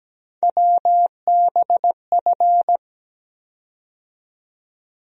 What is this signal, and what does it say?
Morse code, heard as a keyed tone, about 17 wpm, 700 Hz. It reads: WBF
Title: USA Call Sign Postfixes - 3 Letters 17wpm